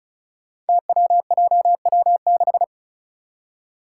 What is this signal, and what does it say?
Morse code, heard as a keyed tone, about 36 wpm, 700 Hz. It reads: TWJW6
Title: Five Challenging Letter-Number Combinations 35wpm